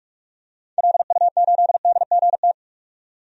Morse code, heard as a keyed tone, about 46 wpm, 700 Hz. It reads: LU8DGT
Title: DXCC-100 Call Signs - 6 Characters - 45wpm